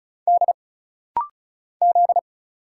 Morse code, heard as a keyed tone, about 33 wpm, 700 Hz. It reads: D Z
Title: Single Letter-Number - Rapid-Fire 35wpm